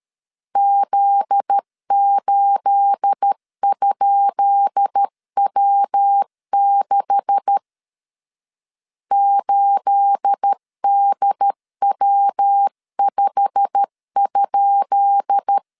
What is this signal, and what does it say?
Morse code, heard as a keyed tone, about 13 wpm, 800 Hz. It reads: Z8?W6 8DW5?
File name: LEZIONI / 10 SIGNS / 10B INT 12wpm 2414